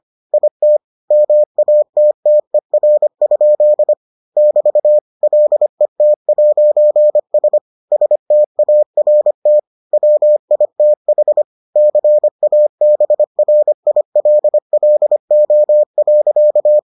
Morse code, heard as a keyed tone, about 25 wpm, 600 Hz. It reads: IT MATTER? <BT> LET'S START WITH CABRILLO.